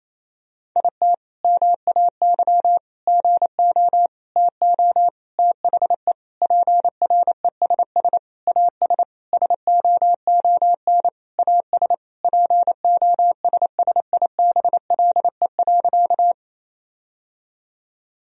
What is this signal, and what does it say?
Morse code, heard as a keyed tone, about 28 wpm, 700 Hz. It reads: IT MAY GO TO THE PRESS AS SOON AS POSSIBLE.
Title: Sentences from Top 1000 Words - Review Part 1 - 28wpm